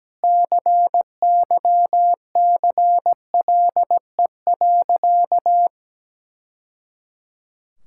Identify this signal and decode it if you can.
Morse code, heard as a keyed tone, about 17 wpm, 700 Hz. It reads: CYCLE.
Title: Sentences from Top 1000 Words - Review Part 6 - 17wpm